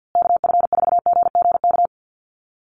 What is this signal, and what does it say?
Morse code, heard as a keyed tone, about 48 wpm, 700 Hz. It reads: KF4GGK
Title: USA Call Signs Repeated 2x 50wpm